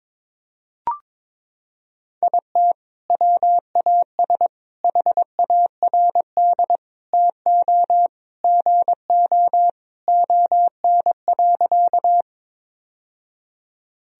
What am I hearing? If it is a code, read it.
Morse code, heard as a keyed tone, about 22 wpm, 700 Hz. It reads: IT WAS HARD TO GO ON.
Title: Sentences from Top 1000 Words - Review Part 1 - 22wpm